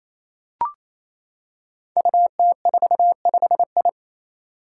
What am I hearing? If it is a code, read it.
Morse code, heard as a keyed tone, about 27 wpm, 700 Hz. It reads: UT45I
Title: Five Challenging Letter-Number Combinations 28wpm